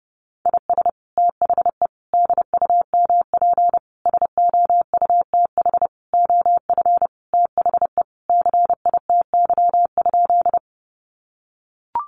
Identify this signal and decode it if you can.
Morse code, heard as a keyed tone, about 28 wpm, 700 Hz. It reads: IS THE DUMP SOUTH OF THE CITY?
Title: Sentences from Top 1300 Words 30wpm